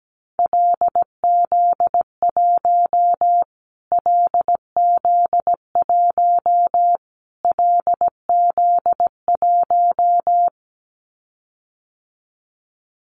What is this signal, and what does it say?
Morse code, heard as a keyed tone, about 17 wpm, 700 Hz. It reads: LZ1 LZ1 LZ1
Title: Callsign Prefixes for Top 25 DXCC - Repeated 3x - 17wpm